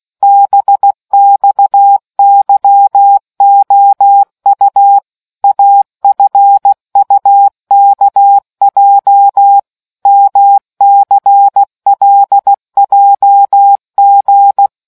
Morse code, heard as a keyed tone, about 16 wpm, 800 Hz. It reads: BXYOU AFUKJ MCLJG